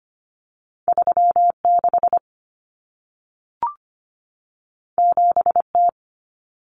Morse code, heard as a keyed tone, about 25 wpm, 700 Hz. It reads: 36 7T
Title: Common Cut Numbers - 2-Digit 25wpm